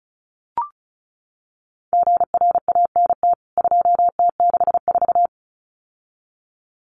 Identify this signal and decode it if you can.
Morse code, heard as a keyed tone, about 34 wpm, 700 Hz. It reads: GRANT 2T64
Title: CWT Exchange 35wpm